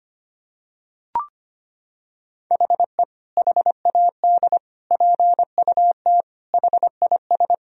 Morse code, heard as a keyed tone, about 25 wpm, 700 Hz. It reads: HE HAD PUT HIS
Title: Sentences from Top 1000 Words - Review Part 3 - 25wpm